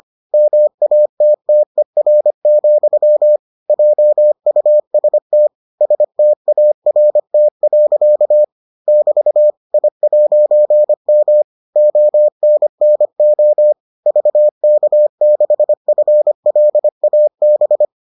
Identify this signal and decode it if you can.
Morse code, heard as a keyed tone, about 25 wpm, 600 Hz. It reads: MATTER, JUST START. <BT> I'M ONNO VK6FLAB